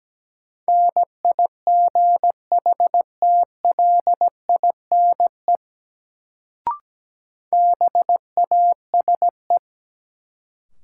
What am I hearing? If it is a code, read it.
Morse code, heard as a keyed tone, about 17 wpm, 700 Hz. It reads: NIGHTLINE BASE
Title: Sets of 5 Words - New Word Drill - 17wpm